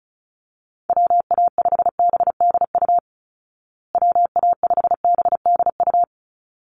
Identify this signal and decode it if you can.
Morse code, heard as a keyed tone, about 34 wpm, 700 Hz. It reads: WA5BDU WA5BDU
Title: Call Signs Repeated 3x - 35wpm